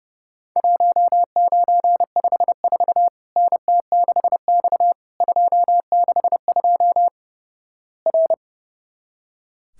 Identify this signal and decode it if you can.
Morse code, heard as a keyed tone, about 29 wpm, 700 Hz. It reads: 1954 NT6X 262 R
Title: Worked All Europe DX Contest - QTC Exchange - 30wpm